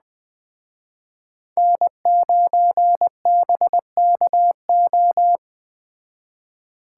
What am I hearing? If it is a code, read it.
Morse code, heard as a keyed tone, about 20 wpm, 700 Hz. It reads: N9BKO
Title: Speed Racing - USA 1x3 Call Signs 17wpm